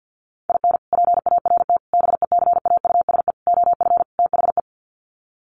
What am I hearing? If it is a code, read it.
Morse code, heard as a keyed tone, about 48 wpm, 700 Hz. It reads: IN PART BECAUSE OF THE
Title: Sets of 5 Words Encore2 50wpm